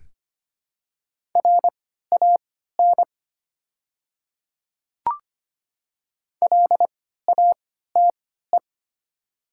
Morse code, heard as a keyed tone, about 25 wpm, spaced as for 14 wpm, 690 Hz. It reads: RAN LATE
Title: 3x Character Spacing - Top 500 Words - 25wpm